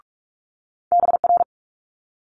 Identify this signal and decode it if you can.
Morse code, heard as a keyed tone, about 46 wpm, 700 Hz. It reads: BR
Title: Multi-letter Phonemes Quick Fire 45wpm